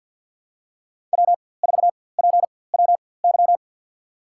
Morse code, heard as a keyed tone, about 46 wpm, spaced as for 23 wpm, 700 Hz. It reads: W4PWY
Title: USA Call Signs - 5 Characters - 4x Character Spacing - 50wpm